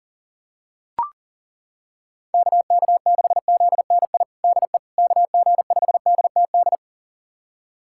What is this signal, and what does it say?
Morse code, heard as a keyed tone, about 40 wpm, 700 Hz. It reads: KK6ZNI DE KG5DTD
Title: USA Callsign DE Callsign - 6 Characters - 40wpm